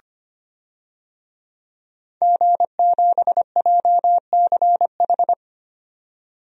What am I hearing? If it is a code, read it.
Morse code, heard as a keyed tone, about 25 wpm, 700 Hz. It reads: G7JCH
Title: DXCC-100 Call Signs - 5 Characters - 25wpm